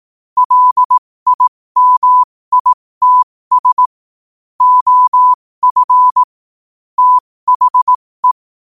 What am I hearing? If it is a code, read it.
Morse code, heard as a keyed tone, about 18 wpm, spaced as for 15 wpm, 1000 Hz. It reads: LIMITS OF THE
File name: qotdfiles / 15wpm